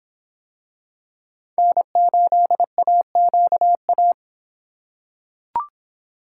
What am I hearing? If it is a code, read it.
Morse code, heard as a keyed tone, about 26 wpm, 700 Hz. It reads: N8AQA